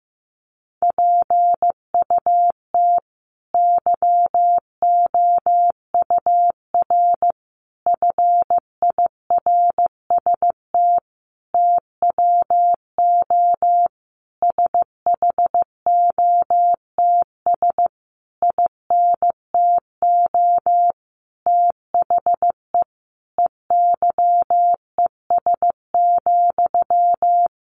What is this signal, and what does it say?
Morse code, heard as a keyed tone, about 15 wpm, 700 Hz. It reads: PUT YOUR FIRST TWO SHOTS INTO THE EYES,
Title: A Sound Of Thunder 15wpm